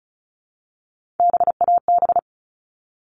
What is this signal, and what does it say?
Morse code, heard as a keyed tone, about 34 wpm, 700 Hz. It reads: BAB
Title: All Cut Numbers - 3-Digit 35wpm